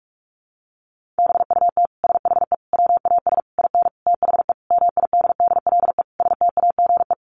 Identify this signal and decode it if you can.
Morse code, heard as a keyed tone, about 44 wpm, 700 Hz. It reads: BUT SHE WAS IN THE MIDDLE STAGE
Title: Sentences from Top 1200 Words 45wpm